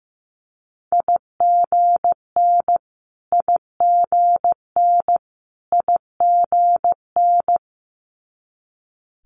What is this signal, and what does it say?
Morse code, heard as a keyed tone, about 15 wpm, 700 Hz. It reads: IGN IGN IGN
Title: Multi-letter Phonemes Repeated 3x 15wpm